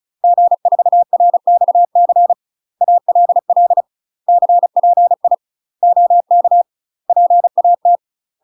Morse code, heard as a keyed tone, about 36 wpm, 700 Hz. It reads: G4RXC ALL CPI OK PAT
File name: UK MASTER QSOs TRAINING / 35WPM / UK MASTER QSOs 0028